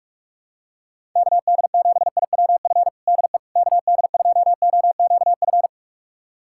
Morse code, heard as a keyed tone, about 42 wpm, 700 Hz. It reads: KD7IWF DE KD2OQF
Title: USA Callsign DE Callsign - 6 Characters - 45wpm